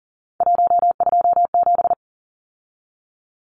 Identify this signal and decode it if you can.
Morse code, heard as a keyed tone, about 38 wpm, 700 Hz. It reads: 127
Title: Numbers - 3-Digit 40wpm